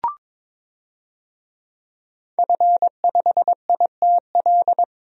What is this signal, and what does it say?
Morse code, heard as a keyed tone, about 22 wpm, 700 Hz. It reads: F5ITL
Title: DXCC-100 Call Signs - 5 Characters - 22wpm